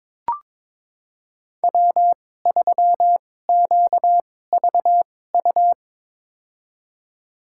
Morse code, heard as a keyed tone, about 22 wpm, spaced as for 16 wpm, 700 Hz. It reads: W3QVU
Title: USA Call Signs - 5 Characters - 2x Character Spacing - 22wpm